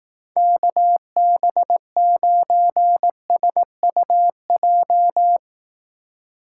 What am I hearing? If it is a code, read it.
Morse code, heard as a keyed tone, about 18 wpm, 700 Hz. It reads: KB9SUJ
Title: Speed Racing - USA 2x3 Call Signs 15wpm